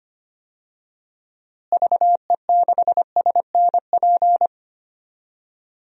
Morse code, heard as a keyed tone, about 25 wpm, 700 Hz. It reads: VE6SNP
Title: DXCC-100 Call Signs - 6 Characters - Repeated 2x - 25wpm